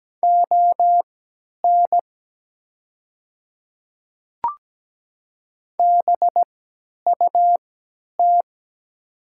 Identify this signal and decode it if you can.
Morse code, heard as a keyed tone, about 17 wpm, spaced as for 10 wpm, 690 Hz. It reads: ON BUT
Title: 3x Character Spacing - Top 100 Words - 17wpm